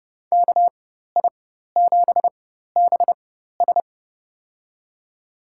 Morse code, heard as a keyed tone, about 30 wpm, spaced as for 14 wpm, 700 Hz. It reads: KI7BS
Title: USA Call Signs - 5 Characters - 4x Character Spacing - 30wpm